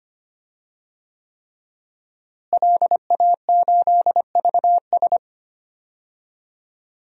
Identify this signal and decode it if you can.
Morse code, heard as a keyed tone, about 25 wpm, 700 Hz. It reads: LA8VS
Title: DXCC-100 Call Signs - 5 Characters - Repeated 3x - 25wpm